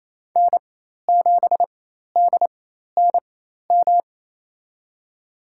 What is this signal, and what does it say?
Morse code, heard as a keyed tone, about 28 wpm, spaced as for 13 wpm, 700 Hz. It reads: N7DNM
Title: USA Call Signs - 5 Characters - 4x Character Spacing - 28wpm